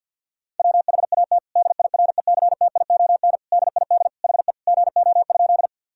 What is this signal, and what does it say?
Morse code, heard as a keyed tone, about 50 wpm, 700 Hz. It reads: WHAT DIRECTION DID HE GO?
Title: Sentences from Top 1000 Words - Review Part 5 - 50wpm